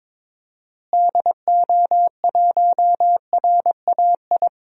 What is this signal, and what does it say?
Morse code, heard as a keyed tone, about 22 wpm, 700 Hz. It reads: DO1RAI
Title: DXCC-100 Call Signs - 6 Characters - Repeated 2x - 22wpm